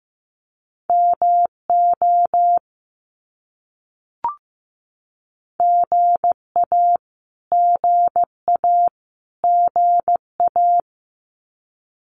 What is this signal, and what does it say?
Morse code, heard as a keyed tone, about 15 wpm, 700 Hz. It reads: MO GA GA GA
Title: US States Repeated 3x 15wpm